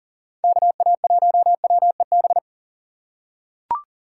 Morse code, heard as a keyed tone, about 40 wpm, 700 Hz. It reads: KA1WEB
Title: USA Call Signs - 6 Characters - 8x Character Spacing - 40wpm